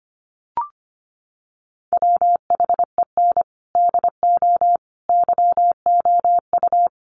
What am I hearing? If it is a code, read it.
Morse code, heard as a keyed tone, about 25 wpm, 700 Hz. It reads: WHEN DO YOU
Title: Sentences from Top 400 Words 25wpm